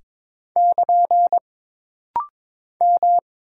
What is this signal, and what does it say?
Morse code, heard as a keyed tone, about 22 wpm, 700 Hz. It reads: <KN> M
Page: This rapid-fire timing is geared to push your ICR (Instant Character Recognition) skills.